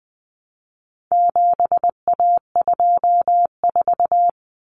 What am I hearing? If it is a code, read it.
Morse code, heard as a keyed tone, about 20 wpm, 700 Hz. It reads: 7A24
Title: Common Cut Numbers - 4-Digit 20wpm